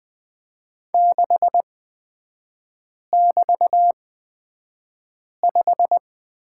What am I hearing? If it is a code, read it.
Morse code, heard as a keyed tone, about 20 wpm, spaced as for 5 wpm, 700 Hz. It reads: 6<BT>5
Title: 224 Lesson 36e - BT - Three Characters 20wpm